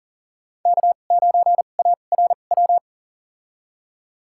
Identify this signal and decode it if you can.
Morse code, heard as a keyed tone, about 40 wpm, spaced as for 29 wpm, 700 Hz. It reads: K9ARW
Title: USA Call Signs - 5 Characters - 2x Character Spacing - 40wpm